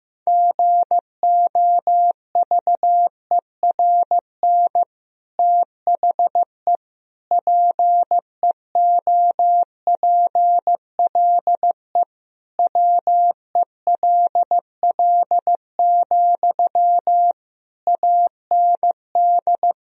Morse code, heard as a keyed tone, about 15 wpm, 700 Hz. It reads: GOVERN THE PEOPLE WELL, AND
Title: Sentences from Top 1000 Words - Review Part 2 - 15wpm